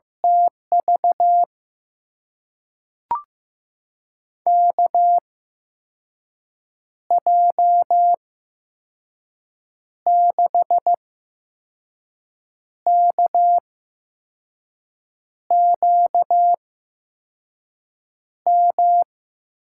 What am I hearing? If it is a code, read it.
Morse code, heard as a keyed tone, about 15 wpm, spaced as for 5 wpm, 700 Hz. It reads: TV KJ6KQM
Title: USA Call Signs - 6 Characters - 8x Character Spacing - 15wpm